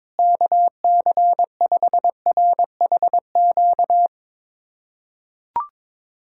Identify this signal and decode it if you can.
Morse code, heard as a keyed tone, about 22 wpm, 700 Hz. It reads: KC5RHQ
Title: USA Callsign DE Callsign 22wpm